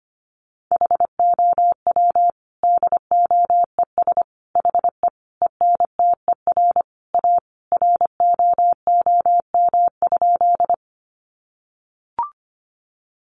Morse code, heard as a keyed tone, about 25 wpm, 700 Hz. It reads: HOW DOES HE ENTER A ROOM?
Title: Sentences from Top 1300 Words 25wpm